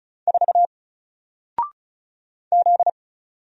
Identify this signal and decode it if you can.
Morse code, heard as a keyed tone, about 36 wpm, 700 Hz. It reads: 4 Z
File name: Single-Letter-Number-35wpm